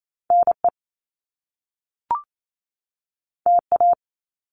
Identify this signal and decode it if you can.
Morse code, heard as a keyed tone, about 28 wpm, 700 Hz. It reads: NE TA
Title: Top Two Letter Combinations in Words 28wpm